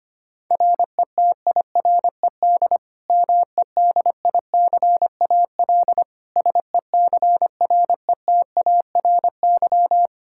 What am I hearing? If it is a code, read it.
Morse code, heard as a keyed tone, about 25 wpm, 700 Hz. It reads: RETIRED MEDICAL SECRETARY
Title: Practice QSO - Exchange 3 - More Info - 25wpm